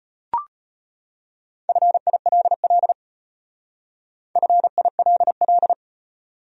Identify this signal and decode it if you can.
Morse code, heard as a keyed tone, about 34 wpm, 700 Hz. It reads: FILL FILL